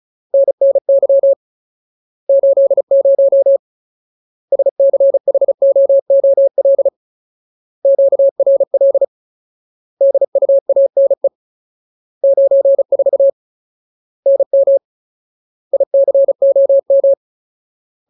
Morse code, heard as a keyed tone, about 35 wpm, 550 Hz. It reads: NNY 80 SCHOOL QRL DUANE 94 NM ICOM
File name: Ultimate Word List / 35WPM Ultimate